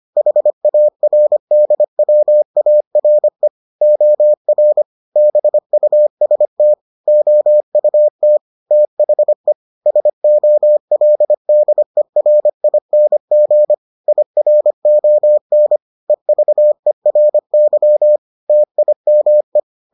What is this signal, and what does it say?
Morse code, heard as a keyed tone, about 25 wpm, 600 Hz. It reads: HARDWARE OR BUST OUT THE SOLDERING IRON EVERY TIME